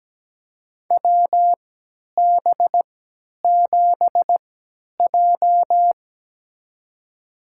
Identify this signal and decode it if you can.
Morse code, heard as a keyed tone, about 17 wpm, spaced as for 10 wpm, 700 Hz. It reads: WB7J
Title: USA Call Signs - 4 Characters - 3x Character Spacing - 17wpm